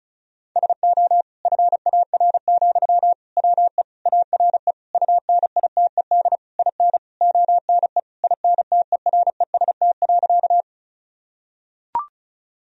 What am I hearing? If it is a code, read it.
Morse code, heard as a keyed tone, about 36 wpm, 700 Hz. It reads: SO FAR, WE ARE UNITED IN ONE INTEREST.
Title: Sentences from Top 1000 Words - Review Part 4 - 35wpm